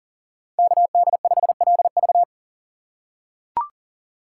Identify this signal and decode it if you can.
Morse code, heard as a keyed tone, about 40 wpm, 700 Hz. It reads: KD5LV
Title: USA Call Signs - 5 Characters - 6x Character Spacing - 40wpm